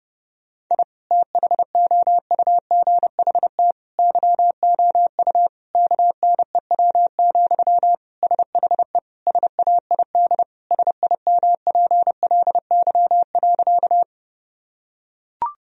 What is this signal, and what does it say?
Morse code, heard as a keyed tone, about 30 wpm, 700 Hz. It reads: I THOUGHT YOU KNEW, SHE SAID SIMPLY.
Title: Sentences from Top 1000 Words - Review Part 4 - 30wpm